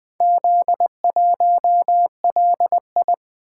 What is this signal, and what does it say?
Morse code, heard as a keyed tone, about 20 wpm, 700 Hz. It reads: Z1LI
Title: Four Letter-Number Combinations - 20wpm